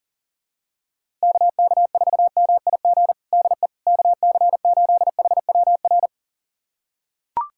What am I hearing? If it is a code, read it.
Morse code, heard as a keyed tone, about 37 wpm, 700 Hz. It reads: KK4MIG DE KC8HWR
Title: USA Callsign DE Callsign 40wpm